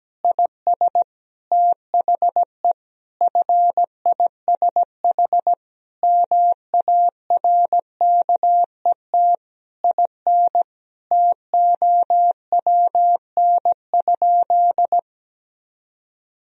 Morse code, heard as a keyed tone, about 17 wpm, 700 Hz. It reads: IS THE FISH MARKET IN TOWN?
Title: Sentences from Top 1000 Words - Review Part 3 - 17wpm